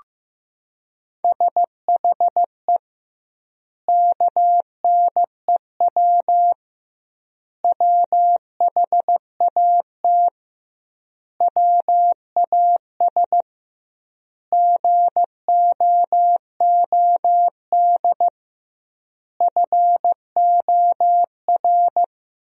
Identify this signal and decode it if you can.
Morse code, heard as a keyed tone, about 16 wpm, 700 Hz. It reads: SHE KNEW WHAT WAS GOOD FOR
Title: 2x Word-Spacing - Sentences from Top 400 Words - 15wpm